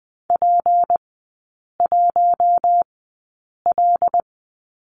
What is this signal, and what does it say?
Morse code, heard as a keyed tone, about 20 wpm, spaced as for 8 wpm, 700 Hz. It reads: P1L
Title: 212 Lesson 34e - Number 8 - Three Characters 20wpm